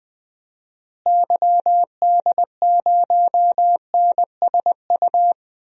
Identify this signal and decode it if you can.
Morse code, heard as a keyed tone, about 20 wpm, 700 Hz. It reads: YD0NSU
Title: DXCC-100 Call Signs - 6 Characters - Repeated 3x - 20wpm